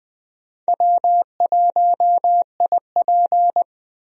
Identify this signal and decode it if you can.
Morse code, heard as a keyed tone, about 21 wpm, 700 Hz. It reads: W1IP
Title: USA Call Signs - 4 Characters - 20wpm